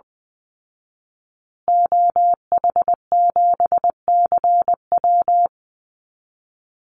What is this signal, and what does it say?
Morse code, heard as a keyed tone, about 20 wpm, 700 Hz. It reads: OH7CW
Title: Call Signs Repeated 2x - 20wpm